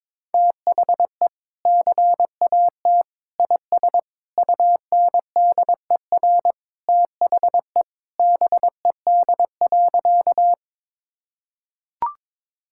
Morse code, heard as a keyed tone, about 22 wpm, 700 Hz. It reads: THE CAT IS UNDER THE BED.
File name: practice / 22wpm / Sentences with 2-Word Phrases - 22wpm